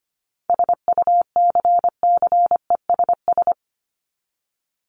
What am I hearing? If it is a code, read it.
Morse code, heard as a keyed tone, about 25 wpm, 700 Hz. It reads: SUCCESS
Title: Top 751-1000 Words 25wpm